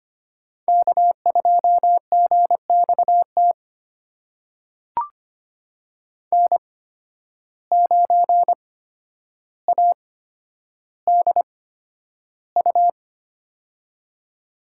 Morse code, heard as a keyed tone, about 25 wpm, spaced as for 11 wpm, 700 Hz. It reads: K2GXT N9ADU
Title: USA Call Signs - 5 Characters - 8x Character Spacing - 25wpm